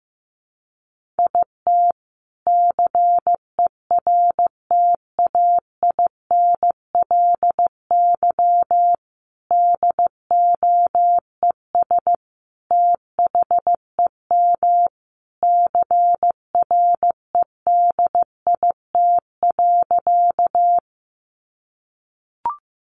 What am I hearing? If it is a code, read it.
Morse code, heard as a keyed tone, about 15 wpm, 700 Hz. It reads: IT CERTAINLY DOES THEM CREDIT.
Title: Sentences from Top 1000 Words 15wpm